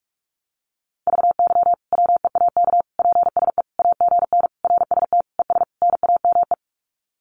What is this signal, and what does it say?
Morse code, heard as a keyed tone, about 44 wpm, 700 Hz. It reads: VY WEAK PSE AGN RST ES NAME
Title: Common QSO Phrases 45wpm